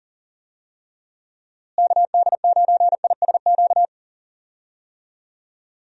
Morse code, heard as a keyed tone, about 37 wpm, 700 Hz. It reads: KD9ISQ
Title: USA Call Signs - 6 Characters - Repeated 3x - 40wpm